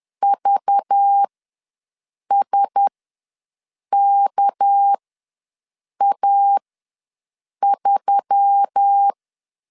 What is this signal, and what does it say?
Morse code, heard as a keyed tone, about 10 wpm, spaced as for 6 wpm, 800 Hz. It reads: VSKA3
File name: LEZIONI / 6 PKX0 / 6 PK ws 10wpm 3745